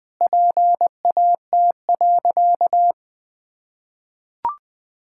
Morse code, heard as a keyed tone, about 20 wpm, 700 Hz. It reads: PAT.
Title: Pangram Challenge v2 - 20wpm